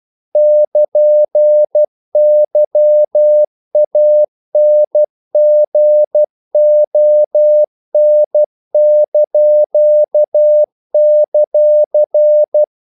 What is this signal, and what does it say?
Morse code, heard as a keyed tone, about 12 wpm, 600 Hz. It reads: <KN>YANGON);
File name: CW / Practice / NationalCapitals12